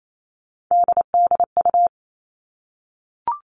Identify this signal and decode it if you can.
Morse code, heard as a keyed tone, about 28 wpm, 700 Hz. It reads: DDU
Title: All Cut Numbers - 3-Digit 28wpm